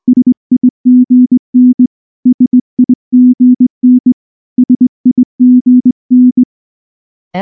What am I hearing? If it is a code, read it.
Morse code, heard as a keyed tone, about 19 wpm, 250 Hz. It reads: SIGN SIGN SIGN